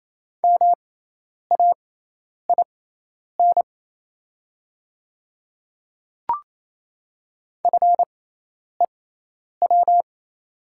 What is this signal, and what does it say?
Morse code, heard as a keyed tone, about 28 wpm, spaced as for 10 wpm, 700 Hz. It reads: MAIN FEW
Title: 6x Character Spacing - Top 500 Words - 28wpm